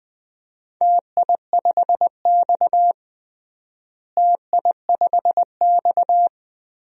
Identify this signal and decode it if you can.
Morse code, heard as a keyed tone, about 19 wpm, 700 Hz. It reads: TI5X TI5X
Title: DXCC-100 Call Signs - 4 Characters - Repeated 3x - 20wpm